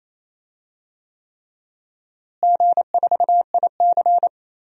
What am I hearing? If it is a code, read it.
Morse code, heard as a keyed tone, about 28 wpm, 700 Hz. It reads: G4IC
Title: DXCC-100 Call Signs - 4 Characters - Repeated 3x - 28wpm